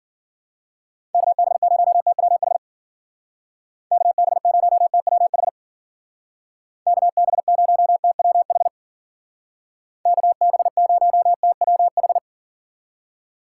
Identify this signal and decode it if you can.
Morse code, heard as a keyed tone, about 53 wpm, 700 Hz. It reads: KB0TWH KB0TWH KB0TWH KB0TWH